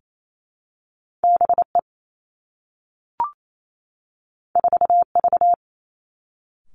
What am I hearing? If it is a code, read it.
Morse code, heard as a keyed tone, about 28 wpm, 700 Hz. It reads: B E 4 V
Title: Two Challenging Letter-Number Combinations 28wpm